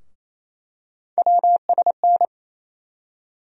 Morse code, heard as a keyed tone, about 27 wpm, 700 Hz. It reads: WSN
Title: USA Call Sign Postfixes - 3 Letters 28wpm